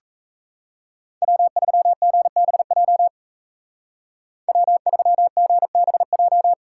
Morse code, heard as a keyed tone, about 34 wpm, 700 Hz. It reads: W3GBJ W3GBJ
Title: Speed Racing - USA 1x3 Call Signs 28wpm